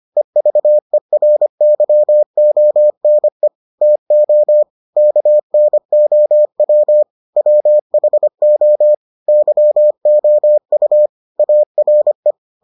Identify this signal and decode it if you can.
Morse code, heard as a keyed tone, about 25 wpm, 600 Hz. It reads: EVERYONE TO KNOW WHO YOU ARE